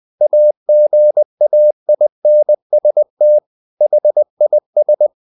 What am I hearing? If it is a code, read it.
Morse code, heard as a keyed tone, about 20 wpm, 600 Hz. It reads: AGAINST HIS
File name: the old man and the sea part 2 20wpm